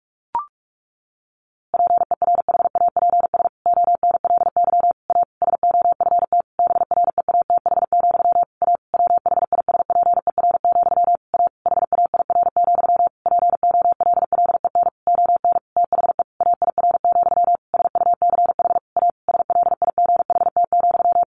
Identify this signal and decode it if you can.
Morse code, heard as a keyed tone, about 44 wpm, 700 Hz. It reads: PERHAPS ONLY A SOFT BREATH, A WHISPER, A HAIR, POLLEN ON THE AIR, SUCH A SLIGHT,
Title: A Sound Of Thunder 45wpm